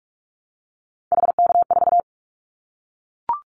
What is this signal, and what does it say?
Morse code, heard as a keyed tone, about 44 wpm, 700 Hz. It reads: HK4
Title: Callsign Prefixes for Top 100 DXCC - 45wpm